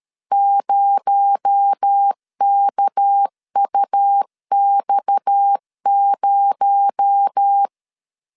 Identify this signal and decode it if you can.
Morse code, heard as a keyed tone, about 13 wpm, 800 Hz. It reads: 0KUX0
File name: LEZIONI / 6 PKX0 / 6 PKX0 12wpm 2231